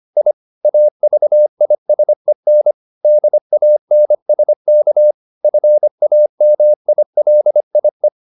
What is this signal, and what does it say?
Morse code, heard as a keyed tone, about 25 wpm, 600 Hz. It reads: I AVISEN DANSK FAMILIE
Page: Ferie 25 wpm